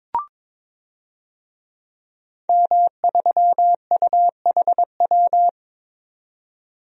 Morse code, heard as a keyed tone, about 22 wpm, 700 Hz. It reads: M3UHW
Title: DXCC-100 Call Signs - 5 Characters - 22wpm